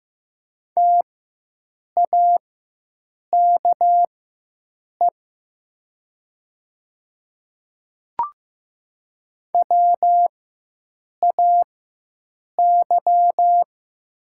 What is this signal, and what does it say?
Morse code, heard as a keyed tone, about 15 wpm, spaced as for 7 wpm, 700 Hz. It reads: TAKE WAY
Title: 4x Character Spacing - Top 200 Words - 15wpm